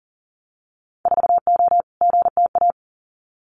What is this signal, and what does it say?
Morse code, heard as a keyed tone, about 40 wpm, 700 Hz. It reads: 4O GTA
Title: Winter Field Day Exchanges 40wpm